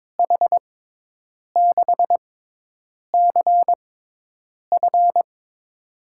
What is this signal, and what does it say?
Morse code, heard as a keyed tone, about 22 wpm, spaced as for 8 wpm, 700 Hz. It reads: H6CF
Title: USA Call Signs - 5 Characters - 6x Character Spacing - 22wpm